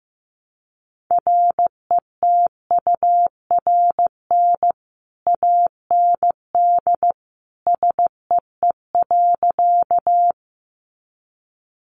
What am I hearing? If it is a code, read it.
Morse code, heard as a keyed tone, about 15 wpm, 700 Hz. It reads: RETURN AND SEE.
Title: Sentences from Top 800 Words 15wpm